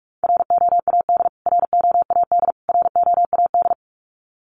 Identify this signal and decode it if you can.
Morse code, heard as a keyed tone, about 44 wpm, 700 Hz. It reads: ROAD ROAD ROAD
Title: Top 500 Words Repeated 45wpm